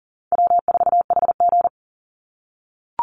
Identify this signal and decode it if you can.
Morse code, heard as a keyed tone, about 40 wpm, 700 Hz. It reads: W4HG
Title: Easy Call Signs - 40wpm